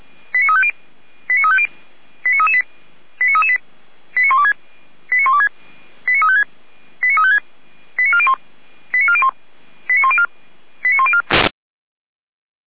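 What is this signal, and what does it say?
Five-tone selective call (ZVEI), heard as quick runs of five tones, tones of about 70 ms. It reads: 89350 89350 89338 89338 89126 89126 89356 89356 89401 89401 89104 89104
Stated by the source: Five-tone ZVEI selective call received by SDR, from feuerwehr-orlamuende.org fire brigade audio archive.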